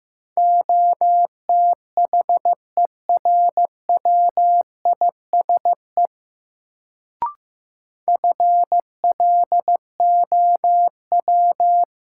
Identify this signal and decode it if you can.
Morse code, heard as a keyed tone, about 15 wpm, 700 Hz. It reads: OTHERWISE FLOW
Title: Sentences from Top 1200 Words - New Word Drill - 15wpm